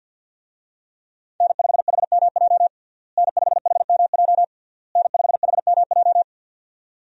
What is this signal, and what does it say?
Morse code, heard as a keyed tone, about 49 wpm, 700 Hz. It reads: N5HMJ N5HMJ N5HMJ
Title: USA Call Signs - 5 Characters - Repeated 3x - 50wpm